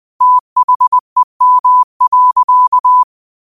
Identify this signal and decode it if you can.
Morse code, heard as a keyed tone, about 20 wpm, 1000 Hz. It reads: THEM.